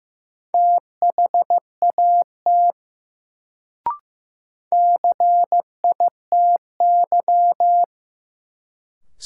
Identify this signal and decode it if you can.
Morse code, heard as a keyed tone, about 15 wpm, 700 Hz. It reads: THAT CITY
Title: Sets of 2 Words - New Word Drill - 15wpm